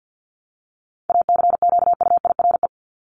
Morse code, heard as a keyed tone, about 50 wpm, 700 Hz. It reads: ACQUIRE
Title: Top 1501-1750 Words 50wpm